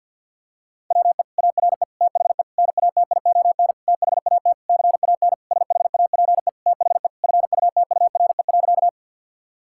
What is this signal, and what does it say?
Morse code, heard as a keyed tone, about 50 wpm, 700 Hz. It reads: WE ARE THE NATION THAT CAN SHAPE THE FUTURE.
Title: Sentences from Top 1000 Words - Review Part 6 - 50wpm